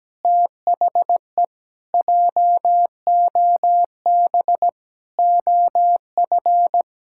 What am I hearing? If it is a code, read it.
Morse code, heard as a keyed tone, about 17 wpm, 700 Hz. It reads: THE JOB OF
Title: Pangram Challenge v2 - 17wpm